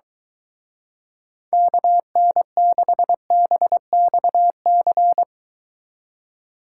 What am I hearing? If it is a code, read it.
Morse code, heard as a keyed tone, about 22 wpm, 700 Hz. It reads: KN6BXC